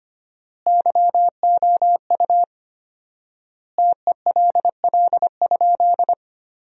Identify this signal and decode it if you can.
Morse code, heard as a keyed tone, about 25 wpm, 700 Hz. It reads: YOU TELL?
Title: 4x Word-Spacing - Sentences from Top 300 Words - 25wpm